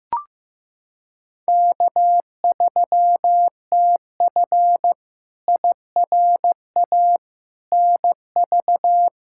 Morse code, heard as a keyed tone, about 15 wpm, 700 Hz. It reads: K3TF IRA NV
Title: K1USN SST QSO Practice - 15wpm